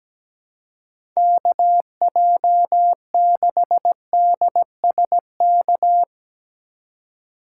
Morse code, heard as a keyed tone, about 17 wpm, 700 Hz. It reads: KJ6DSK
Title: USA Call Signs - 6 Characters - 2x Character Spacing - 17wpm